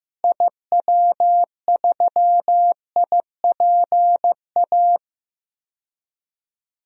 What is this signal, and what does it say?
Morse code, heard as a keyed tone, about 15 wpm, 700 Hz. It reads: IW3IPA
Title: DXCC-100 Call Signs - 6 Characters - Repeated 3x - 15wpm